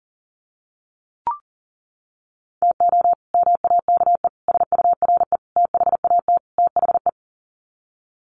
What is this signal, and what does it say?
Morse code, heard as a keyed tone, about 40 wpm, 700 Hz. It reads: TO MAKE SURE THAT THE
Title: Sets of 5 Words 40wpm